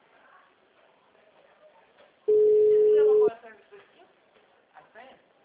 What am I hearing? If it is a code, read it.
Ringback tone (Europe, 425 Hz), with speech.